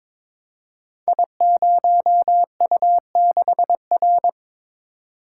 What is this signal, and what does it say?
Morse code, heard as a keyed tone, about 22 wpm, 700 Hz. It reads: I0U6R
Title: Code Groups - 5 Letter:Numbers - 22wpm